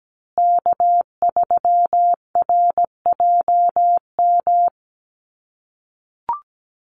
Morse code, heard as a keyed tone, about 17 wpm, 700 Hz. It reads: K3RJM
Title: USA Call Signs Repeated 2x 17wpm